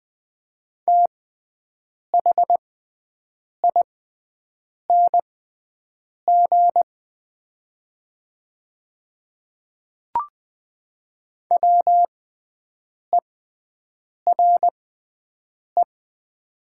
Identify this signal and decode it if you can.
Morse code, heard as a keyed tone, about 20 wpm, spaced as for 7 wpm, 700 Hz. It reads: THING WERE
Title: 6x Character Spacing - Top 100 Words - 20wpm